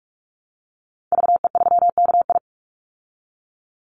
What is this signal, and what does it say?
Morse code, heard as a keyed tone, about 44 wpm, 700 Hz. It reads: VE3KI
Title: Call Signs Repeated 2x - 45wpm